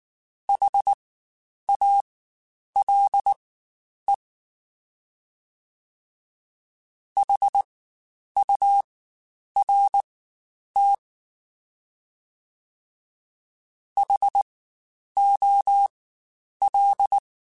Morse code, heard as a keyed tone, about 19 wpm, spaced as for 9 wpm, 800 Hz. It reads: HALE HURT HOL
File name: Spaced Alphas / Spaced Alphas mp3s / Hs sp @18